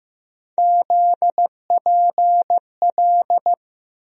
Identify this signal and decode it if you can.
Morse code, heard as a keyed tone, about 15 wpm, 700 Hz. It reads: ZPL
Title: USA Call Sign Postfixes - 3 Letters 15wpm